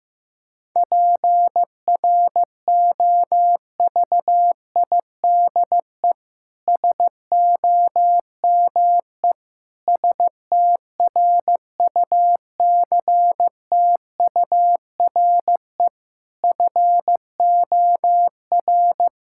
Morse code, heard as a keyed tone, about 15 wpm, 700 Hz. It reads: PROVIDE SOME STRUCTURE FOR
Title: Sentences from Top 1000 Words - Review Part 6 - 15wpm